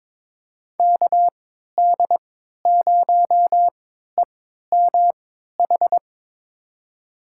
Morse code, heard as a keyed tone, about 22 wpm, spaced as for 12 wpm, 700 Hz. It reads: KD0EMH
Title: USA Call Signs - 6 Characters - 3x Character Spacing - 22wpm